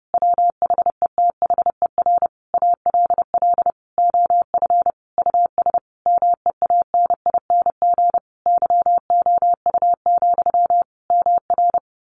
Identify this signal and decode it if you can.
Morse code, heard as a keyed tone, about 30 wpm, 700 Hz. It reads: WHETHER ALL OF US MEANING YOU, MR
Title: A Sound Of Thunder 30wpm